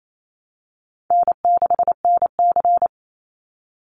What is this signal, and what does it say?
Morse code, heard as a keyed tone, about 28 wpm, 700 Hz. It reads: N6NC
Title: Call Signs Repeated 3x - 28wpm